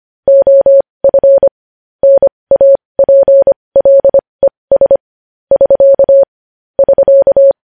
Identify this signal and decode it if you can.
Morse code, heard as a keyed tone, about 25 wpm, spaced as for 20 wpm, 550 Hz. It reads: OF NAPLES <SK> <SK>
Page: News Headlines in Morse Code at 20 WPM